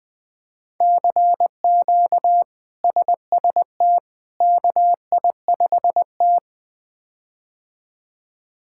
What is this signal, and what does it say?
Morse code, heard as a keyed tone, about 20 wpm, 700 Hz. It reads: CQ SST KI5T
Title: K1USN SST QSO Practice - 20wpm